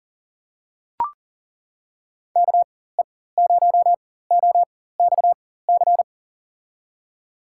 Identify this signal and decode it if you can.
Morse code, heard as a keyed tone, about 40 wpm, spaced as for 19 wpm, 700 Hz. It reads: KE0OXC
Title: USA Call Signs - 6 Characters - 4x Character Spacing - 40wpm